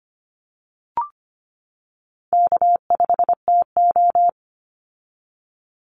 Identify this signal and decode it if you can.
Morse code, heard as a keyed tone, about 25 wpm, 700 Hz. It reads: K5TO
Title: Easy Call Signs - 25wpm